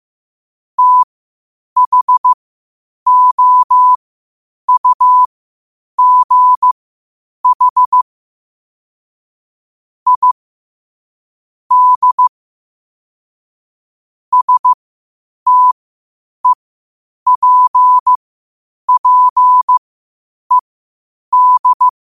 Morse code, heard as a keyed tone, about 15 wpm, spaced as for 8 wpm, 1000 Hz. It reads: THOUGH I D STEPPED
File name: qotdfiles / 8wpm